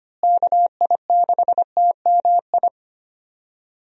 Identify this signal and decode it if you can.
Morse code, heard as a keyed tone, about 25 wpm, 700 Hz. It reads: KI6TMI
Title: USA Call Signs - 6 Characters - Repeated 3x - 25wpm